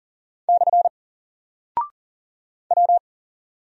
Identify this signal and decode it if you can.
Morse code, heard as a keyed tone, about 40 wpm, 700 Hz. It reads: / W
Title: Single Character (Letter:Number:Punctuation:Prosign) Alt 40wpm